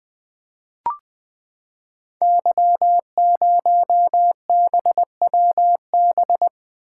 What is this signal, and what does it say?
Morse code, heard as a keyed tone, about 20 wpm, 700 Hz. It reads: Y0BWB
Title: Code Groups - 5 Letter:Numbers - 20wpm